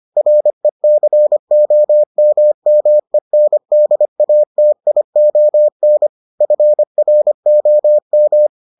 Morse code, heard as a keyed tone, about 25 wpm, 600 Hz. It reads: RECOMMENDATION FROM